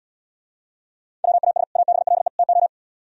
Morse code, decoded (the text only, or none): WAS ABLE TO